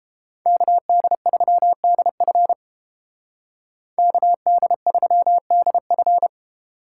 Morse code, heard as a keyed tone, about 30 wpm, 700 Hz. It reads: KD3DF KD3DF